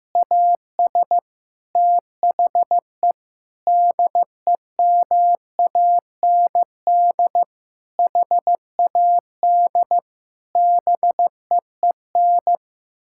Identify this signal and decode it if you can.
Morse code, heard as a keyed tone, about 16 wpm, 700 Hz. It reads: AS THE DEMAND HAD BEEN
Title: Sentences from Top 1000 Words - Review Part 6 - 15wpm